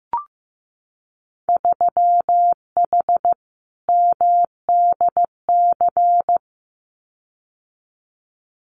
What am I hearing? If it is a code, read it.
Morse code, heard as a keyed tone, about 15 wpm, 700 Hz. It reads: 3H MDC
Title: Winter Field Day Exchanges Encore 15wpm